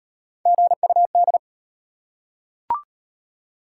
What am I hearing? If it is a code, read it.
Morse code, heard as a keyed tone, about 33 wpm, 700 Hz. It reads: GUD